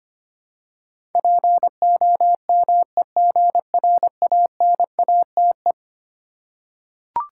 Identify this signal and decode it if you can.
Morse code, heard as a keyed tone, about 25 wpm, 700 Hz. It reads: POMEGRANATE
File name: practice / 25wpm / Difficult Words with Hints - 25wpm